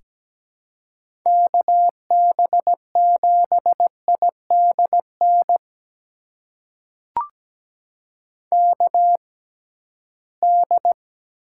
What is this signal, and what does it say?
Morse code, heard as a keyed tone, about 17 wpm, spaced as for 13 wpm, 700 Hz. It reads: KB7IDN KD
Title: USA Call Signs - 6x Character Spacing - 17wpm